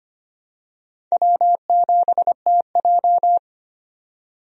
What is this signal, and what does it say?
Morse code, heard as a keyed tone, about 25 wpm, 700 Hz. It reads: W7TJ
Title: USA Call Signs - 4 Characters - 25wpm